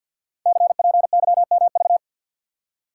Morse code, heard as a keyed tone, about 53 wpm, 700 Hz. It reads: CPYMV
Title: Code Groups - 5 Letter:Numbers - 50wpm